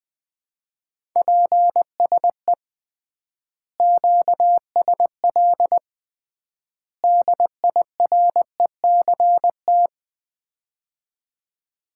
Morse code, heard as a keyed tone, about 20 wpm, 700 Hz. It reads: PSE QSL DIRECT
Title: Common QSO Phrases - 3x Word-Spacing - 20wpm